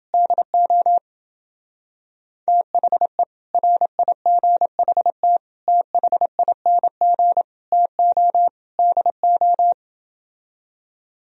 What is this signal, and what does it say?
Morse code, heard as a keyed tone, about 29 wpm, 700 Hz. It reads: DO THE RIGHT THING TO DO